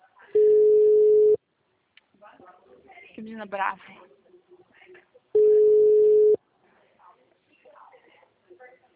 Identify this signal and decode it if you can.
Ringback tone (Europe, 425 Hz), with speech.
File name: out- -2006- - - .1788